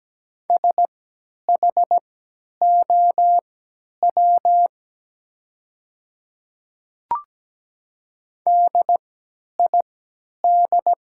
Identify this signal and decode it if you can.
Morse code, heard as a keyed tone, about 17 wpm, spaced as for 10 wpm, 690 Hz. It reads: SHOW DID
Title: 3x Character Spacing - Top 300 Words - 17wpm